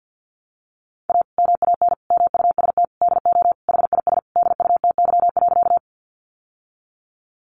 Morse code, heard as a keyed tone, about 50 wpm, 700 Hz. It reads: A MAN MUST DO HIS DUTY.
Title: Sentences from Top 1500 Words 50wpm